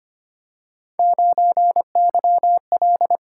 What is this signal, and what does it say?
Morse code, heard as a keyed tone, about 25 wpm, 700 Hz. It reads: 9YL
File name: Three Letter-Number Combinations - 25wpm